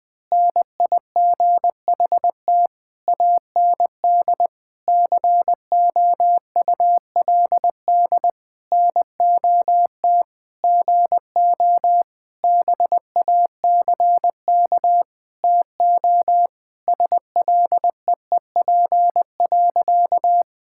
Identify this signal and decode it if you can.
Morse code, heard as a keyed tone, about 19 wpm, 700 Hz. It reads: NIGHT AND COULD NOT GO BACK TO SLEEP.
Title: Sentences with 7-Word Phrases - Spoken Beginning - 20wpm